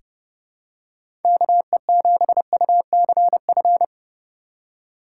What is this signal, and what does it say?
Morse code, heard as a keyed tone, about 31 wpm, 700 Hz. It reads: KE7UCF
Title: USA Call Signs - 6 Characters - 3x Character Spacing - 30wpm